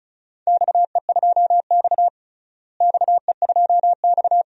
Morse code, heard as a keyed tone, about 36 wpm, 700 Hz. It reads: XE2X XE2X
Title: DXCC-100 Call Signs - 4 Characters - Repeated 2x - 35wpm